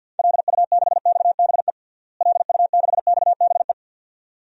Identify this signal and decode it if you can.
Morse code, heard as a keyed tone, about 50 wpm, 700 Hz. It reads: LU6XBE LU6XBE
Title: DXCC-100 Call Signs - 6 Characters - Repeated 2x - 50wpm